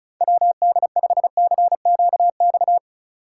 Morse code, heard as a keyed tone, about 45 wpm, 700 Hz. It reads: WD5CQX